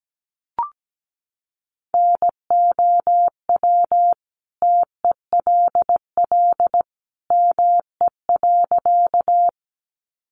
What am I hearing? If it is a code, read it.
Morse code, heard as a keyed tone, about 17 wpm, 700 Hz. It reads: NOW TELL ME.
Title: Sentences from Top 200 Words 17wpm